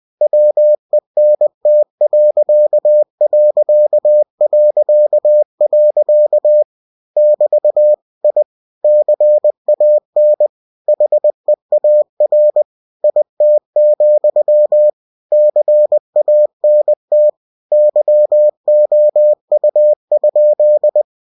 Morse code, heard as a keyed tone, about 20 wpm, 600 Hz. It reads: WENT.... <BT> I CAN HEAR IT, CANT YOU?
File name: cw-books / winnie-the-pooh / winnie-20wpm-page-0043